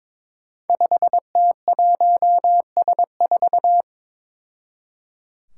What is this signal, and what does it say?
Morse code, heard as a keyed tone, about 22 wpm, 700 Hz. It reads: 5T1S4
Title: Five Challenging Letter-Number Combinations 22wpm